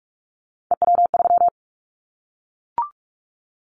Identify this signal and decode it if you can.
Morse code, heard as a keyed tone, about 44 wpm, 700 Hz. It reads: EW3
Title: Callsign Prefixes for Top 100 DXCC - 45wpm